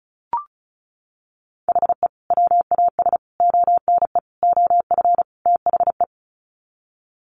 Morse code, heard as a keyed tone, about 34 wpm, 700 Hz. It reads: HE WAS ONE OF THE
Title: Sets of 5 Words 35wpm